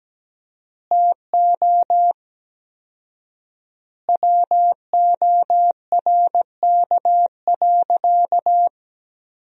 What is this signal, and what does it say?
Morse code, heard as a keyed tone, about 17 wpm, 700 Hz. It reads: TO WORK.
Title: 4x Word-Spacing - Sentences from Top 300 Words - 17wpm